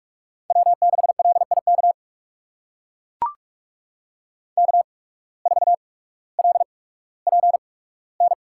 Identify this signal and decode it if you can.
Morse code, heard as a keyed tone, about 42 wpm, spaced as for 19 wpm, 700 Hz. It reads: W6LIK K4LPN
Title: USA Call Signs - 5 Characters - 8x Character Spacing - 45wpm